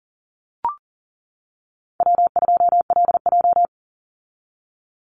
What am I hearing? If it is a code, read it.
Morse code, heard as a keyed tone, about 40 wpm, 700 Hz. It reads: W2LJ
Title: Call Signs - 40wpm